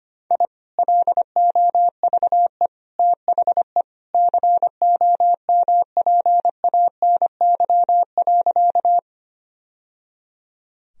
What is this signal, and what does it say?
Morse code, heard as a keyed tone, about 25 wpm, 700 Hz. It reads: I LOVE THE COMPANY.
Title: Sentences from Top 1000 Words - Review Part 3 - 25wpm